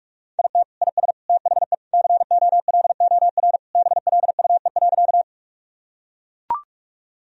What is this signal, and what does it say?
Morse code, heard as a keyed tone, about 41 wpm, 700 Hz. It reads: IT IS THE COLOR BLUE.
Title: Sentences from Top 1000 Words - Review Part 3 - 45wpm